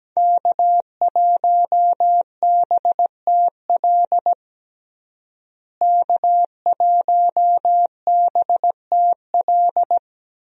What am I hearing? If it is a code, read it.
Morse code, heard as a keyed tone, about 17 wpm, 700 Hz. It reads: K1BTL K1BTL
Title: USA Call Signs - 5 Characters - Repeated 2x - 17wpm